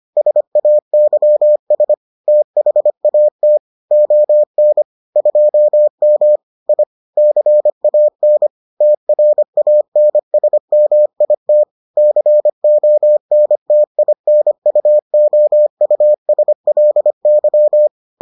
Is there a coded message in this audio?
Yes: Morse code from 0.2 to 17.9 s, heard as a keyed tone, about 25 wpm, 600 Hz. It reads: SAYS THAT ON 2M I CAN TRANSMIT CONTINUOUSLY